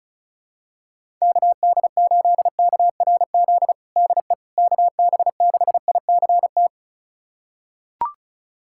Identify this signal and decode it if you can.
Morse code, heard as a keyed tone, about 36 wpm, 700 Hz. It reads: KD8KRZ DE KB6ICT
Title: USA Callsign DE Callsign 35wpm